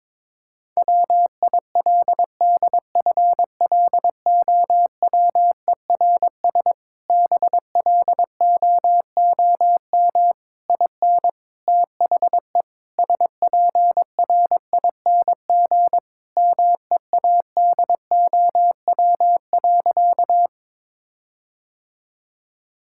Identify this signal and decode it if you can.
Morse code, heard as a keyed tone, about 22 wpm, 700 Hz. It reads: WILDFLOWERS BLOOM IN THE SPRING MEADOW.
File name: Difficult Sentences with Hints - 22wpm